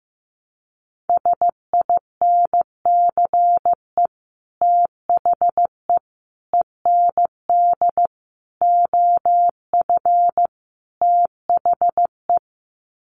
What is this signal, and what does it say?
Morse code, heard as a keyed tone, about 15 wpm, 700 Hz. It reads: SINCE THE END OF THE
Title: Sets of 5 Words Encore 15wpm